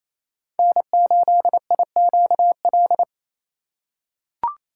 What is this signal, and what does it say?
Morse code, heard as a keyed tone, about 28 wpm, 700 Hz. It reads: N8IQL
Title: USA Call Signs - 3x Character Spacing - 28wpm